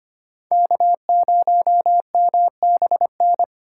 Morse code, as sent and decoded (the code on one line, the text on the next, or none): -.- ----- -- -... -.
K0MBN